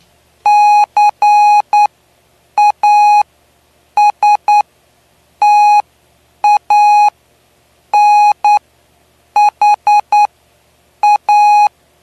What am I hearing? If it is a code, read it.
Morse code, heard as a keyed tone, about 9 wpm, spaced as for 7 wpm, 830 Hz. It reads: CASTANHA
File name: cw / py2lcd / 09-PY2LCD-Li��o 9